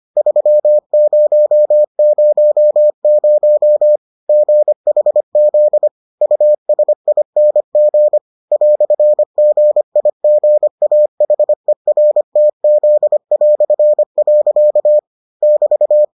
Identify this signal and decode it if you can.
Morse code, heard as a keyed tone, about 25 wpm, 600 Hz. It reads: 3000 GHZ USING "GIGAHERTZ". <BT>